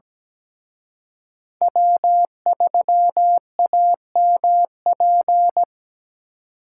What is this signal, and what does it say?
Morse code, heard as a keyed tone, about 20 wpm, 700 Hz. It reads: W3AMP
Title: Speed Racing - USA 1x3 Call Signs 17wpm